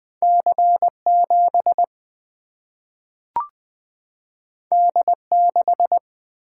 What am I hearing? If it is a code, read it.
Morse code, heard as a keyed tone, about 20 wpm, 700 Hz. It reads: C 7 D 6
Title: Single-Letter Single-Number 20wpm